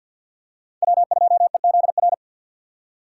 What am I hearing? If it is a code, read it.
Morse code, heard as a keyed tone, about 50 wpm, 700 Hz. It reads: W2EZR
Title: USA Call Signs - 5 Characters - 2x Character Spacing - 50wpm